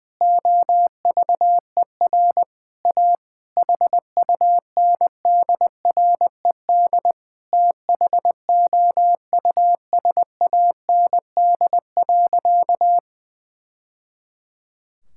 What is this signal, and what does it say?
Morse code, heard as a keyed tone, about 20 wpm, 700 Hz. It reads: OVER A HUNDRED THOUSAND.
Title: Sentences from Top 1000 Words - Review Part 5 - 20wpm